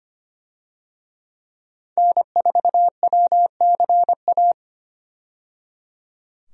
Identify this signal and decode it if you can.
Morse code, heard as a keyed tone, about 25 wpm, 700 Hz. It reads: N4WCA
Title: USA Call Signs - 5 Characters - 25wpm